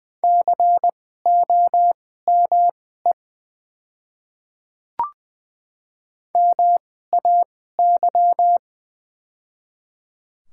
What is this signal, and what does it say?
Morse code, heard as a keyed tone, about 20 wpm, spaced as for 14 wpm, 700 Hz. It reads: COME MAY
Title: 2x Character Spacing - Top 100 Words - 20wpm